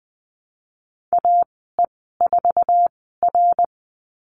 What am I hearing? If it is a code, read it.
Morse code, heard as a keyed tone, about 20 wpm, spaced as for 15 wpm, 700 Hz. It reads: AE4R
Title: 069 Lesson 12g - Letter d - Call Signs 20wpm